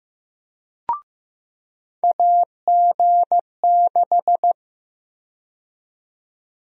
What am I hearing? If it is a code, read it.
Morse code, heard as a keyed tone, about 15 wpm, 700 Hz. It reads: AG6
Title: USA Call Sign Prefixes - 15wpm